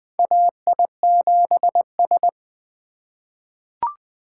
Morse code, heard as a keyed tone, about 20 wpm, 700 Hz. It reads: AI7S
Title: USA Call Signs - 4 Characters - 4x Character Spacing - 20wpm